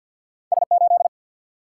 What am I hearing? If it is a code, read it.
Morse code, heard as a keyed tone, about 50 wpm, 700 Hz. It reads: S 8
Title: Single-Letter Single-Number 50wpm